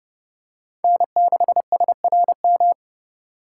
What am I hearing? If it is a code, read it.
Morse code, heard as a keyed tone, about 29 wpm, 700 Hz. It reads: N6SRM